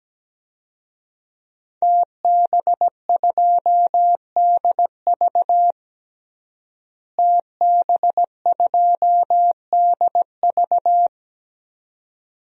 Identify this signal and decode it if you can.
Morse code, heard as a keyed tone, about 17 wpm, 700 Hz. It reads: TB2DV TB2DV
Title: DXCC-100 Call Signs - 5 Characters - Repeated 2x - 17wpm